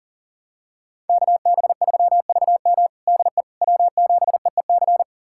Morse code, heard as a keyed tone, about 40 wpm, 700 Hz. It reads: KB3VM DE W7EEC
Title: USA Callsign DE Callsign - 5 Characters - 40wpm